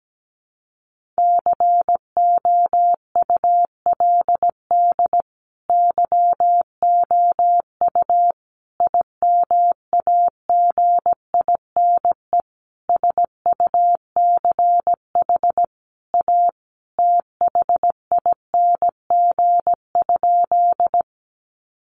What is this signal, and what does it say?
Morse code, heard as a keyed tone, about 17 wpm, 700 Hz. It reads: COULD YOU IMAGINE SUCH A THING?
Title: Sentences from Top 1400 Words 17wpm